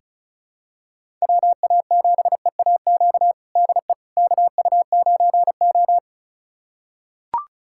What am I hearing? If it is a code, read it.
Morse code, heard as a keyed tone, about 36 wpm, 700 Hz. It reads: WA7EAQ DE KU9O
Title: USA Callsign DE Callsign 35wpm